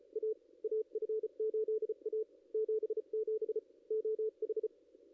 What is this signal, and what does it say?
Morse code, heard as a keyed tone, about 37 wpm, spaced as for 32 wpm, 440 Hz. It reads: A AF8A 77 OH
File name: contests / ss-cw-k5tr-op-n5rz -2300